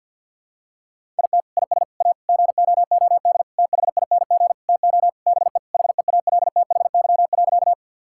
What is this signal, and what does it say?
Morse code, heard as a keyed tone, about 52 wpm, 700 Hz. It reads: IT IS A GOOD THING TO BE HEALTHY.
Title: Sentences from Top 1000 Words - Review Part 5 - 50wpm